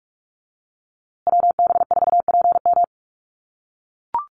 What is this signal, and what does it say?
Morse code, heard as a keyed tone, about 44 wpm, 700 Hz. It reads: WB4PM
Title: USA Call Signs Repeated 2x 45wpm